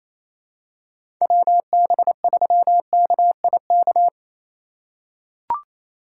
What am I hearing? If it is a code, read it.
Morse code, heard as a keyed tone, about 28 wpm, 700 Hz. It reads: WB3KIK
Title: USA Call Signs - 2x Character Spacing - 28wpm